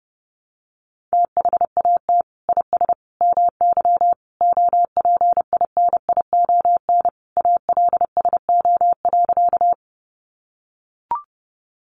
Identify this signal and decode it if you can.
Morse code, heard as a keyed tone, about 30 wpm, 700 Hz. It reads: THAT IS MY OPINION ALSO.
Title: Sentences from Top 1200 Words 30wpm